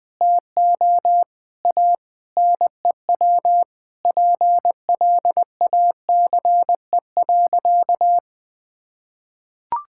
Morse code, heard as a keyed tone, about 20 wpm, 700 Hz. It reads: TO A NEW PLACE.
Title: Sentences with 2-Word Phrases - Spoken Beginning - 20wpm